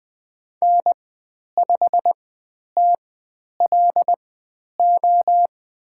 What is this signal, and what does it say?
Morse code, heard as a keyed tone, about 20 wpm, spaced as for 10 wpm, 700 Hz. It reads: N5TLO
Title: 102 Lesson 17g - Letter u - Call Signs 20wpm